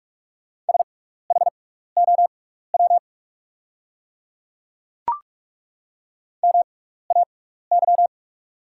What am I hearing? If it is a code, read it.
Morse code, heard as a keyed tone, about 43 wpm, spaced as for 15 wpm, 690 Hz. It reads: SHOW MAY
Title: 6x Character Spacing - Top 500 Words - 45wpm